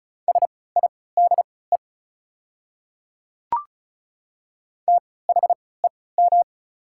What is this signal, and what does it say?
Morse code, heard as a keyed tone, about 34 wpm, spaced as for 20 wpm, 690 Hz. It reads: SIDE THEM
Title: 3x Character Spacing - Top 300 Words - 35wpm